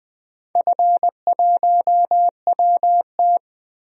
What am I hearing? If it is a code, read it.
Morse code, heard as a keyed tone, about 20 wpm, 700 Hz. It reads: F1WT
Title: DXCC-100 Call Signs - 4 Characters - Repeated 3x - 20wpm